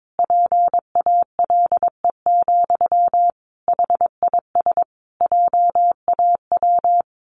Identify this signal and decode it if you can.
Morse code, heard as a keyed tone, about 22 wpm, 700 Hz. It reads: PALE, HIS JAW
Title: A Sound Of Thunder 22wpm